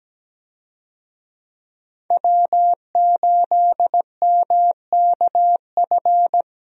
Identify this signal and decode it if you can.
Morse code, heard as a keyed tone, about 17 wpm, 700 Hz. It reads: W8MKF
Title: USA Call Signs - 5 Characters - Repeated 2x - 17wpm